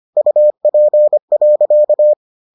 Morse code, heard as a keyed tone, about 25 wpm, 600 Hz. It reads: UP.